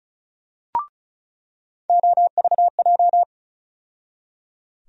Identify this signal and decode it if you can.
Morse code, heard as a keyed tone, about 34 wpm, 700 Hz. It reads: OVJ
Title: USA Call Sign Postfixes - 3 Letters 35wpm